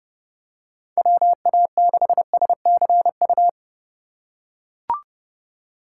Morse code, heard as a keyed tone, about 30 wpm, 700 Hz. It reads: WA6SCU